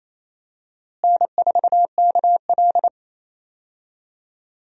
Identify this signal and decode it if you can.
Morse code, heard as a keyed tone, about 28 wpm, 700 Hz. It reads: N4KL
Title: USA Call Signs - 4 Characters - Repeated 2x - 28wpm